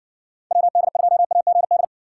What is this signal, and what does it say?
Morse code, heard as a keyed tone, about 60 wpm, 700 Hz. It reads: WD2AGD